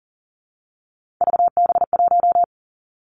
Morse code, heard as a keyed tone, about 40 wpm, 700 Hz. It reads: VB1
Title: Three Challenging Letter-Number Combinations 40wpm